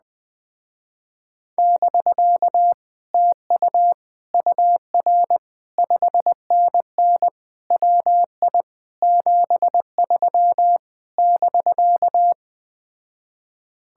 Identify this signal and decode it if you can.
Morse code, heard as a keyed tone, about 20 wpm, 700 Hz. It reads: <BK> TU UR 5NN WI 73 <BK>
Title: Speed-Racing Common QSO Phrases - 17wpm